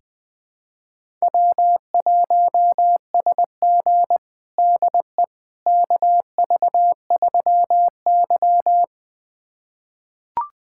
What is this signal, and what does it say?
Morse code, heard as a keyed tone, about 20 wpm, 700 Hz. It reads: W1SG DE KV3Y
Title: USA Callsign DE Callsign - 4 Characters - 20wpm